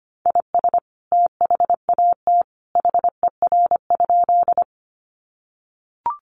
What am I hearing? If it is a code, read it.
Morse code, heard as a keyed tone, about 25 wpm, 700 Hz. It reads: IS THAT HER?
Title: Sentences from Top 100 Words 25wpm